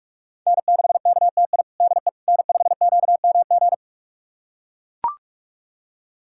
Morse code, heard as a keyed tone, about 43 wpm, 700 Hz. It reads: N6KTI DE N5QMG
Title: USA Callsign DE Callsign - 5 Characters - 45wpm